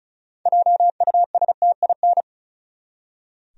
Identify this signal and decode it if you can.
Morse code, heard as a keyed tone, about 36 wpm, 700 Hz. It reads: JUSTIN
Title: USA Common Names - 35wpm